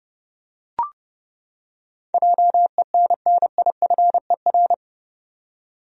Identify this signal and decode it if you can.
Morse code, heard as a keyed tone, about 30 wpm, 700 Hz. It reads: JENNIFER
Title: USA Common Names - 30wpm